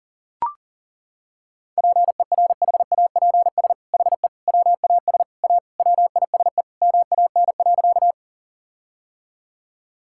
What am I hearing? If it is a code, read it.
Morse code, heard as a keyed tone, about 40 wpm, 700 Hz. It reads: PERHAPS HE WAS A WISE MAN.
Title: Sentences from Top 1600 Words 40wpm